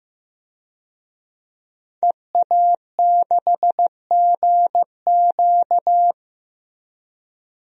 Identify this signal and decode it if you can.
Morse code, heard as a keyed tone, about 15 wpm, 700 Hz. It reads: EA6GQ
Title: DXCC-100 Call Signs - 5 Characters - Repeated 2x - 15wpm